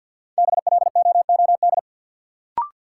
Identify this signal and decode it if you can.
Morse code, heard as a keyed tone, about 51 wpm, 700 Hz. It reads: BLOOD
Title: Sentences from Top 1100 Words - New Word Drill Alt - 50wpm